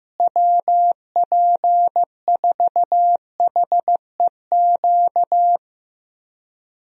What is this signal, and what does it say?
Morse code, heard as a keyed tone, about 16 wpm, 700 Hz. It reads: WP4HEQ
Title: USA Call Signs - 6 Characters - Repeated 2x - 15wpm